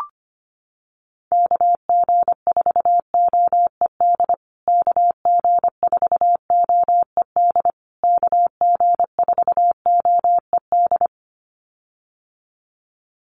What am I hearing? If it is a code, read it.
Morse code, heard as a keyed tone, about 25 wpm, 700 Hz. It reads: KG4OED KG4OED KG4OED
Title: USA Call Signs Repeated 3x 25wpm